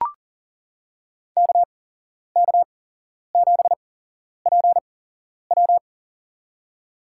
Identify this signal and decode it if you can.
Morse code, heard as a keyed tone, about 40 wpm, spaced as for 11 wpm, 700 Hz. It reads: KK7PW
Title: USA Call Signs - 5 Characters - 8x Character Spacing - 40wpm